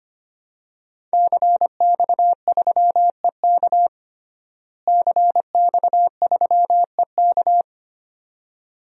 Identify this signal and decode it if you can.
Morse code, heard as a keyed tone, about 25 wpm, 700 Hz. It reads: CX3EK CX3EK
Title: DXCC-100 Call Signs - 5 Characters - Repeated 2x - 25wpm